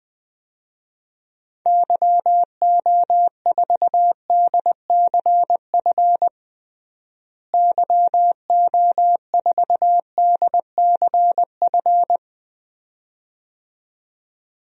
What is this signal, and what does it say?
Morse code, heard as a keyed tone, about 20 wpm, 700 Hz. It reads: YO4DCF YO4DCF
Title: DXCC-100 Call Signs - 6 Characters - Repeated 2x - 20wpm